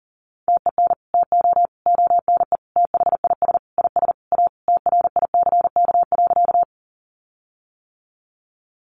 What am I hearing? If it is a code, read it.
Morse code, heard as a keyed tone, about 40 wpm, 700 Hz. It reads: TEN TO ONE THIS IS A TRICK.
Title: Sentences from Top 1500 Words 40wpm